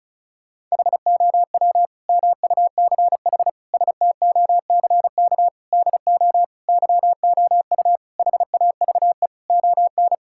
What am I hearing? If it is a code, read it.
Morse code, heard as a keyed tone, about 34 wpm, 700 Hz. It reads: HOW MUCH STOCK DO YOU HAVE ON
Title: Sentences from Top 1000 Words - Review Part 5 - 35wpm